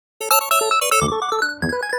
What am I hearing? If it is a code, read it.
There is no code.